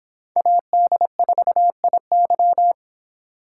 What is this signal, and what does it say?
Morse code, heard as a keyed tone, about 25 wpm, 700 Hz. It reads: AD4IY